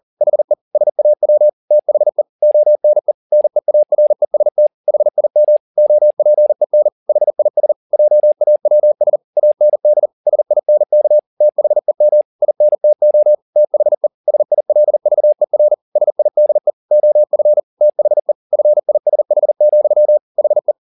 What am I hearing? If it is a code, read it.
Morse code, heard as a keyed tone, about 40 wpm, 600 Hz. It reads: HE SAW THE ONE NEAREST HIM OPEN HIS JAWS AND SINK THEM INTO THE SILVER SIDE OF THE FISH, HE
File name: the old man and the sea 2 40wpm